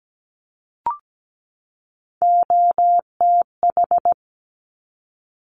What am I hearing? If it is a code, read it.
Morse code, heard as a keyed tone, about 17 wpm, 700 Hz. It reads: OTH
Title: Top Three Letter Combinations in Words 17wpm